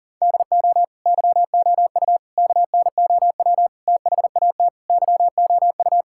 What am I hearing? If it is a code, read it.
Morse code, heard as a keyed tone, about 40 wpm, 700 Hz. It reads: DO YOU KNOW THAT YOU
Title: Sentences from Top 1000 Words - Review Part 5 - 40wpm